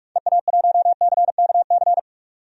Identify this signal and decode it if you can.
Morse code, heard as a keyed tone, about 43 wpm, 700 Hz. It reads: EA1CKC
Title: DXCC-100 Call Signs - 6 Characters - 45wpm